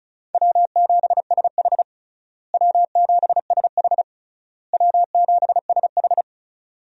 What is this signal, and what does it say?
Morse code, heard as a keyed tone, about 36 wpm, 700 Hz. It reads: W7SH W7SH W7SH
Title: USA Call Signs - 4 Characters - Repeated 3x - 35wpm